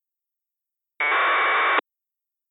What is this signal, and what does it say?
Packet radio (AX.25, as in APRS).